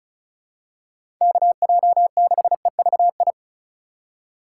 Morse code, heard as a keyed tone, about 36 wpm, 700 Hz. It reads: KJ6EVI
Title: USA Call Signs - 6 Characters - 2x Character Spacing - 35wpm